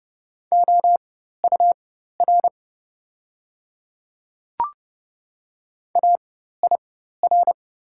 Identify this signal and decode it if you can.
Morse code, heard as a keyed tone, about 30 wpm, spaced as for 14 wpm, 700 Hz. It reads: OUR AIR
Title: 4x Character Spacing - Top 200 Words - 30wpm